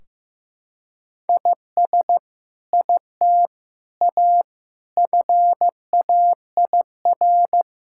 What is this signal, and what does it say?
Morse code, heard as a keyed tone, about 16 wpm, 700 Hz. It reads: IS IT A FAIR
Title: Sentences from Top 1600 Words 15wpm